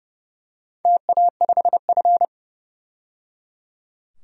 Morse code, heard as a keyed tone, about 30 wpm, 700 Hz. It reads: TA5F
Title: DXCC-100 Call Signs - 4 Characters - 30wpm